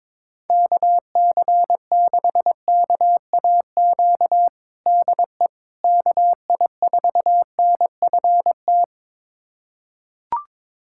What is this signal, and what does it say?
Morse code, heard as a keyed tone, about 22 wpm, 700 Hz. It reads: KC6KAQ DE KI4NFT
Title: USA Callsign DE Callsign - 6 Characters - 22wpm